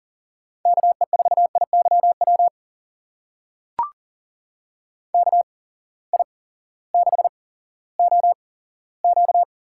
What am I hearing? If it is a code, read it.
Morse code, heard as a keyed tone, about 40 wpm, spaced as for 18 wpm, 700 Hz. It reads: KE4IYW KI6OQ
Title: USA Call Signs - 6 Characters - 8x Character Spacing - 40wpm